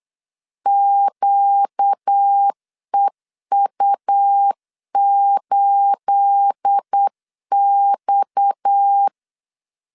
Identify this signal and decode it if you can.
Morse code, heard as a keyed tone, about 8 wpm, 800 Hz. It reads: QEU8X
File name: CW / DET / 1 9 1500C S B 8wpm 5624